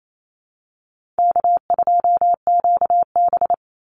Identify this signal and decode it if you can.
Morse code, heard as a keyed tone, about 28 wpm, 700 Hz. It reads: K2QB
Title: Easy Call Signs Repeated 3x - 28wpm